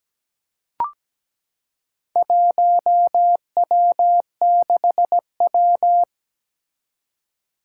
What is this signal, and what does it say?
Morse code, heard as a keyed tone, about 17 wpm, 700 Hz. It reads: 1W6W
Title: Four Challenging Letter-Number Combinations 17wpm